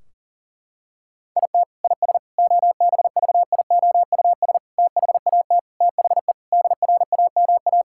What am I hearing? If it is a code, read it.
Morse code, heard as a keyed tone, about 40 wpm, 700 Hz. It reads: IT IS OBVIOUS THAT THE DRAMA
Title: Sentences from Top 1600 Words 40wpm